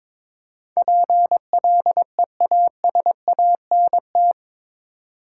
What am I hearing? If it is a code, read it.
Morse code, heard as a keyed tone, about 22 wpm, 700 Hz. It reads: PLEASANT
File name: practice / 22wpm / Eight Letter Words - 22wpm